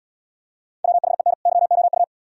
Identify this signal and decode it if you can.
Morse code, heard as a keyed tone, about 80 wpm, 700 Hz. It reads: WHEN YOU